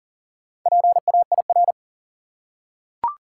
Morse code, heard as a keyed tone, about 40 wpm, 700 Hz. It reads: PAIR
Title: Four Letter Words - 40wpm